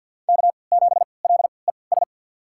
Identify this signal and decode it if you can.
Morse code, heard as a keyed tone, about 52 wpm, spaced as for 29 wpm, 700 Hz. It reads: K7LES
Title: USA Call Signs - 5 Characters - 3x Character Spacing - 50wpm